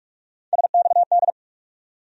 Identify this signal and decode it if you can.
Morse code, heard as a keyed tone, about 46 wpm, 700 Hz. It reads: SXD
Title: USA Call Sign Postfixes - 3 Letters 45wpm